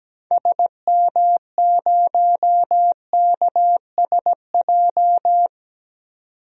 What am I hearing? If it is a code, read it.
Morse code, heard as a keyed tone, about 17 wpm, 700 Hz. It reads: SM0KSJ
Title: DXCC-100 Call Signs - 6 Characters - 17wpm